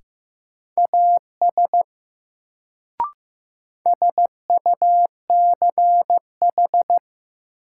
Morse code, heard as a keyed tone, about 15 wpm, 700 Hz. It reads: AS SUCH
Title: Sets of 2 Words - New Word Drill - 15wpm